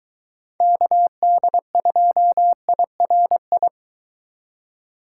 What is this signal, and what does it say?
Morse code, heard as a keyed tone, about 23 wpm, 700 Hz. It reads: KD2IRI